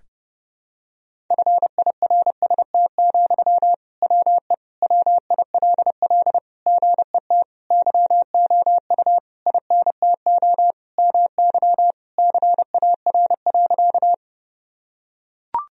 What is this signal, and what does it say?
Morse code, heard as a keyed tone, about 30 wpm, 700 Hz. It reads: FIRST, WE WILL GET YOU INTO MY CAR.
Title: Sentences from Top 1000 Words - Review Part 1 - 30wpm